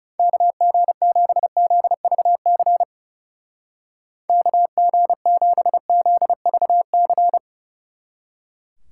Morse code, heard as a keyed tone, about 35 wpm, 700 Hz. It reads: KG7ZVC KG7ZVC
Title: Speed Racing - USA 2x3 Call Signs 30wpm